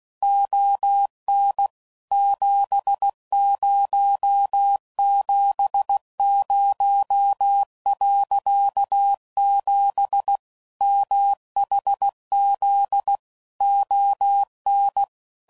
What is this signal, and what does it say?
Morse code, heard as a keyed tone, about 16 wpm, 800 Hz. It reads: ON 7070.7 MHZ ON
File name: CW Code Groups @15WPM / QSOs / WA7QOX QRX